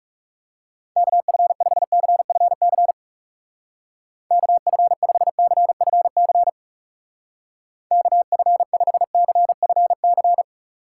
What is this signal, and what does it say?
Morse code, heard as a keyed tone, about 38 wpm, 700 Hz. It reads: KF5CFC KF5CFC KF5CFC